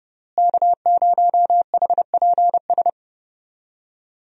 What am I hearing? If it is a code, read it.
Morse code, heard as a keyed tone, about 30 wpm, 700 Hz. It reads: K0HPS
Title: USA Call Signs - 5 Characters - Repeated 3x - 30wpm